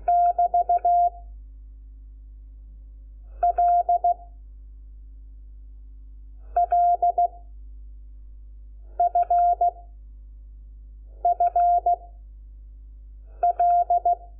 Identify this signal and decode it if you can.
Morse code, heard as a keyed tone, about 16 wpm, 700 Hz. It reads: <BT> L L F F L
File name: Lecciones / CLASE Nº5